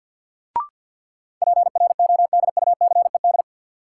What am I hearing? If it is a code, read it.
Morse code, heard as a keyed tone, about 53 wpm, 700 Hz. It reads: PRODUCED
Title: Sentences from Top 1000 Words - Review Part 6 - New Word Drill - 50wpm